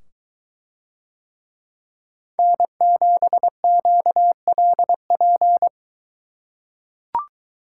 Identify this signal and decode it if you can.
Morse code, heard as a keyed tone, about 22 wpm, 700 Hz. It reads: N7QLP